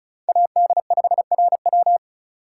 Morse code, heard as a keyed tone, about 36 wpm, 700 Hz. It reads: AD5RW
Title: USA Call Signs - 5 Characters - 3x Character Spacing - 35wpm